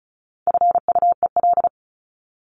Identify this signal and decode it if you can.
Morse code, heard as a keyed tone, about 34 wpm, 700 Hz. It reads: FUEL
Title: Top 1751-2000 Words 35wpm